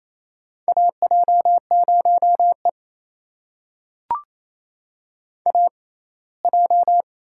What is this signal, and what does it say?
Morse code, heard as a keyed tone, about 28 wpm, spaced as for 19 wpm, 700 Hz. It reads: AJ0E AJ
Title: USA Call Signs - 4 Characters - 6x Character Spacing - 28wpm